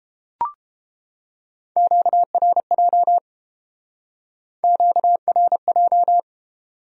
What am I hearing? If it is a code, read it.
Morse code, heard as a keyed tone, about 30 wpm, 700 Hz. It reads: QRJ QRJ